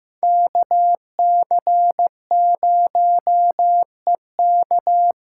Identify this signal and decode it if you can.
Morse code, heard as a keyed tone, about 16 wpm, 700 Hz. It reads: KC0EK
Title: USA Call Signs - 6 Characters - 8x Character Spacing - 15wpm